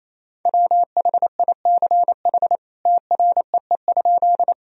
Morse code, heard as a keyed tone, about 28 wpm, 700 Hz. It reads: WHICH TREE?
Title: Sentences from Top 1000 Words - Review Part 1 - 28wpm